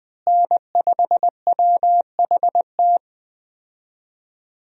Morse code, heard as a keyed tone, about 21 wpm, 700 Hz. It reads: N5WHT
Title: USA Call Signs - 5 Characters - Repeated 2x - 20wpm